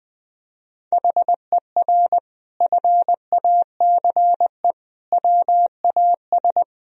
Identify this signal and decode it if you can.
Morse code, heard as a keyed tone, about 21 wpm, 700 Hz. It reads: HER FACE WAS
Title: Sentences from Top 1000 Words - Review Part 2 - 20wpm